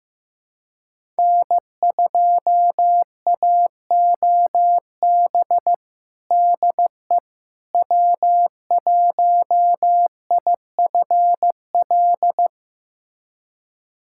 Morse code, heard as a keyed tone, about 15 wpm, 700 Hz. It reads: N2AOB DE W1IFL
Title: USA Callsign DE Callsign - 5 Characters - 15wpm